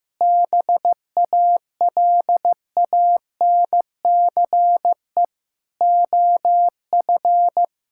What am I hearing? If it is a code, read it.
Morse code, heard as a keyed tone, about 15 wpm, 700 Hz. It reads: BALANCE OF
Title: Sentences from Top 1000 Words - Review Part 6 - 15wpm